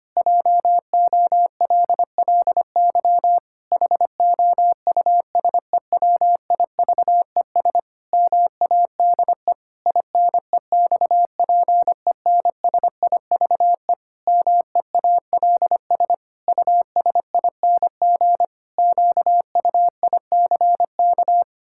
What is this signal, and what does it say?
Morse code, heard as a keyed tone, about 25 wpm, 700 Hz. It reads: JOLLY HOUSEWIVES MADE INEXPENSIVE MEALS USING QUICK
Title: Pangram Challenge v2 - 25wpm